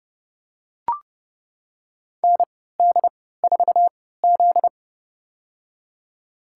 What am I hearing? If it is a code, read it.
Morse code, heard as a keyed tone, about 30 wpm, spaced as for 17 wpm, 700 Hz. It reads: ND4Z
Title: USA Call Signs - 4 Characters - 3x Character Spacing - 30wpm